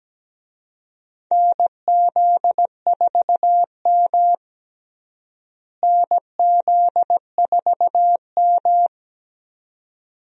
Morse code, heard as a keyed tone, about 17 wpm, 700 Hz. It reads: NZ4M NZ4M
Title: USA Call Signs - 4 Characters - Repeated 2x - 17wpm